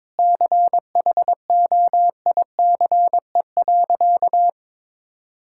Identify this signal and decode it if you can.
Morse code, heard as a keyed tone, about 22 wpm, 700 Hz. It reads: CHOICE.
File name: Sentences with 7-Word Phrases - 22wpm